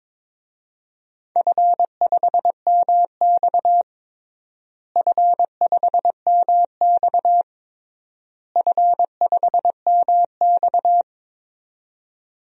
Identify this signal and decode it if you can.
Morse code, heard as a keyed tone, about 22 wpm, 700 Hz. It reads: F5MX F5MX F5MX
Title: DXCC-100 Call Signs - 4 Characters - Repeated 3x - 22wpm